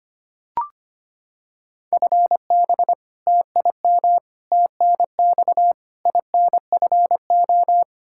Morse code, heard as a keyed tone, about 25 wpm, 700 Hz. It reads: FB TIM TNX INFO
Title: Practice QSO - Exchange 3 - More Info - 25wpm